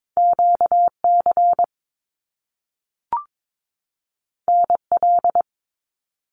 Morse code, heard as a keyed tone, about 22 wpm, 700 Hz. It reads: QC NL
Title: Canadian Provinces and Territories 22wpm